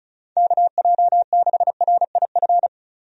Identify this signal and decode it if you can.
Morse code, heard as a keyed tone, about 36 wpm, 700 Hz. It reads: KJ6RIF
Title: USA Call Signs - 6 Characters - 35wpm